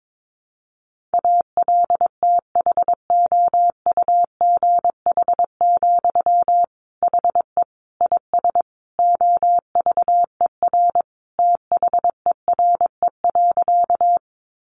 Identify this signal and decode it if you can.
Morse code, heard as a keyed tone, about 22 wpm, 700 Hz. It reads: ALTHOUGH, HE IS OVER THERE.
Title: Sentences from Top 600 Words 22wpm